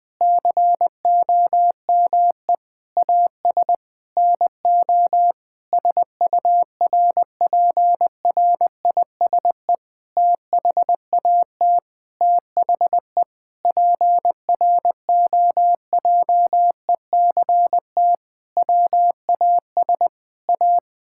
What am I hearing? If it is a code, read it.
Morse code, heard as a keyed tone, about 20 wpm, 700 Hz. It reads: COME AS NO SURPRISE THAT THE PROJECT WAS A
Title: Sentences with 8-Word Phrases - 20wpm